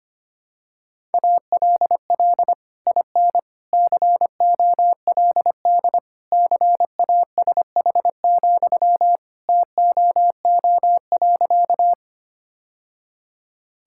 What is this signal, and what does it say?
Morse code, heard as a keyed tone, about 25 wpm, 700 Hz. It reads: ALL IN COLD CASH, TOO.
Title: Sentences from Top 1000 Words - Review Part 5 - 25wpm